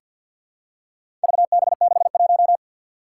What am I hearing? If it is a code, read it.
Morse code, heard as a keyed tone, about 47 wpm, 700 Hz. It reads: VB61
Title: Four Challenging Letter-Number Combinations 50wpm